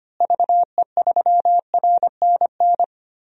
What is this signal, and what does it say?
Morse code, heard as a keyed tone, about 25 wpm, 700 Hz. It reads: VE3RNN
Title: DXCC-100 Call Signs - 6 Characters - Repeated 2x - 25wpm